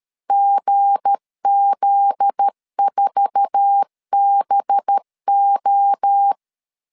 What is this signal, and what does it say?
Morse code, heard as a keyed tone, about 13 wpm, 800 Hz. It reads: GZ4BO
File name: LEZIONI / 8 YJ45 / 8 YJ4 12wpm 2151